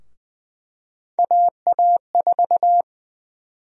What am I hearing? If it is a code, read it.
Morse code, heard as a keyed tone, about 20 wpm, 700 Hz. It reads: AA4
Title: USA Call Sign Prefixes - 20wpm